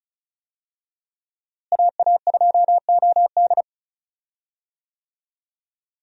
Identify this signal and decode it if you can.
Morse code, heard as a keyed tone, about 34 wpm, 700 Hz. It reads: AA2OD
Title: USA Call Signs - 5 Characters - Repeated 3x - 35wpm